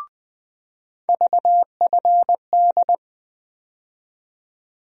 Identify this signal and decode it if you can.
Morse code, heard as a keyed tone, about 20 wpm, 700 Hz. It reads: VFD
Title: USA Call Sign Postfixes - 3 Letters 20wpm